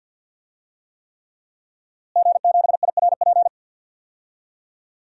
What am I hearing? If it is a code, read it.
Morse code, heard as a keyed tone, about 51 wpm, 700 Hz. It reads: G7IRP
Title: DXCC-100 Call Signs - 5 Characters - Repeated 3x - 50wpm